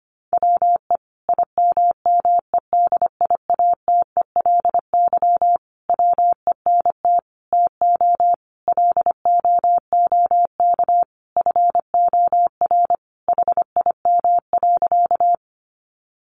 Morse code, heard as a keyed tone, about 25 wpm, 700 Hz. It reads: WE IMMEDIATELY WENT TO LOOK FOR HIM.
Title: Sentences from Top 1100 Words 25wpm